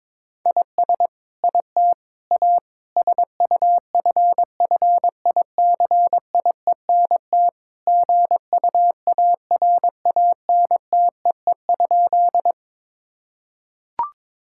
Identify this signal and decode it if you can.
Morse code, heard as a keyed tone, about 22 wpm, 700 Hz. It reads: IS IT A SUFFICIENT GUARANTEE?
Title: Sentences from Top 1600 Words 22wpm